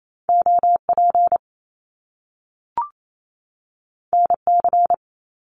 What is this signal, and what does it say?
Morse code, heard as a keyed tone, about 28 wpm, 700 Hz. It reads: OP NC
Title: Top Two Letter Combinations in Words 28wpm